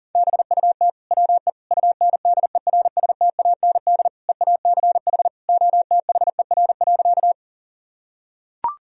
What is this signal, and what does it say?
Morse code, heard as a keyed tone, about 40 wpm, 700 Hz. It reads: BUT WE UNDERSTAND EACH OTHER.
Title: Sentences from Top 1000 Words - Review Part 1 - 40wpm